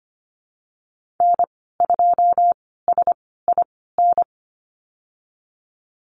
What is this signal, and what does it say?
Morse code, heard as a keyed tone, about 25 wpm, spaced as for 16 wpm, 700 Hz. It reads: N2SIN
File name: Lesson 15g - Number 5 - Call Signs 25wpm